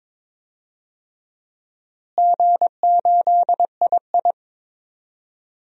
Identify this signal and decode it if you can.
Morse code, heard as a keyed tone, about 22 wpm, 700 Hz. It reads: G8II
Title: DXCC-100 Call Signs - 4 Characters - Repeated 3x - 22wpm